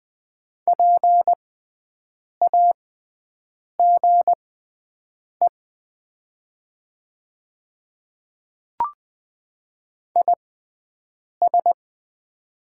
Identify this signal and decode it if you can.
Morse code, heard as a keyed tone, about 20 wpm, spaced as for 7 wpm, 700 Hz. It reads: PAGE IS
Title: 6x Character Spacing - Top 500 Words - 20wpm